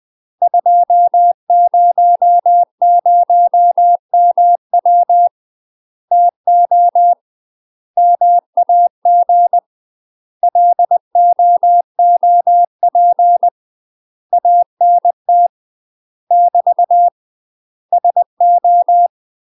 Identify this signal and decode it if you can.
Morse code, heard as a keyed tone, about 20 wpm, 700 Hz. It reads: 200MW TO MAG LOOP ANT <BT> SO
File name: UK MASTER QSOs TRAINING/20WPM